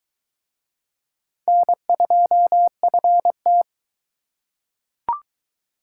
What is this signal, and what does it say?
Morse code, heard as a keyed tone, about 20 wpm, 700 Hz. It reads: N2FT